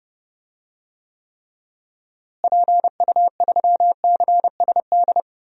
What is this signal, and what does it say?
Morse code, heard as a keyed tone, about 32 wpm, 700 Hz. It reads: PU3CSD
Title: DXCC-100 Call Signs - 6 Characters - Repeated 2x - 30wpm